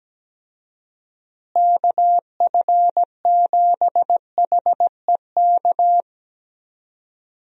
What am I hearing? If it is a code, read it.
Morse code, heard as a keyed tone, about 17 wpm, 700 Hz. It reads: KF7HEK
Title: USA Call Signs - 6 Characters - Repeated 3x - 17wpm